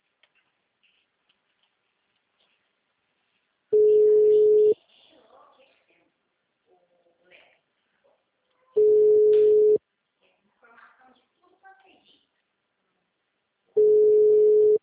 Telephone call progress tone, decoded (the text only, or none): ringback tone